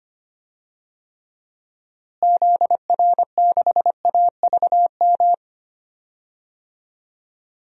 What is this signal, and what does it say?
Morse code, heard as a keyed tone, about 25 wpm, 700 Hz. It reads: ZR6AVM
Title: DXCC-100 Call Signs - 6 Characters - Repeated 3x - 25wpm